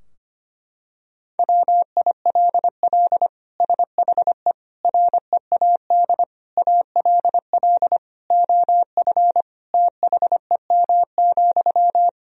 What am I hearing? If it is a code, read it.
Morse code, heard as a keyed tone, about 25 wpm, 700 Hz. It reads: WILL SHE READ ALL OF THEM,
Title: Sentences from Top 1000 Words - Review Part 4 - 25wpm